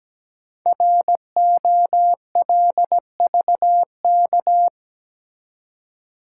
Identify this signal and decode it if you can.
Morse code, heard as a keyed tone, about 17 wpm, 700 Hz. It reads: ROLVK
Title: Code Groups - 5 Letter:Numbers - 17wpm